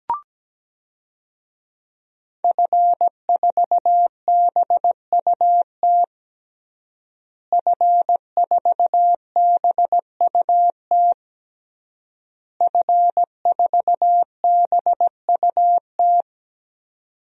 Morse code, heard as a keyed tone, about 17 wpm, 700 Hz. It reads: F4BUT F4BUT F4BUT
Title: DXCC-100 Call Signs - 5 Characters - Repeated 3x - 17wpm